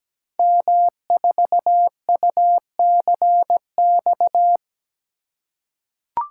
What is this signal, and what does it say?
Morse code, heard as a keyed tone, about 17 wpm, 700 Hz. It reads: M4UCX
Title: Code Groups - 5 Letter:Numbers - 17wpm